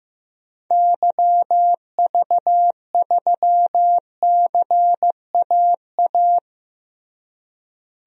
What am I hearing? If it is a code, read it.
Morse code, heard as a keyed tone, about 15 wpm, 700 Hz. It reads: YV3CAA
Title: DXCC-100 Call Signs - 6 Characters - Repeated 2x - 15wpm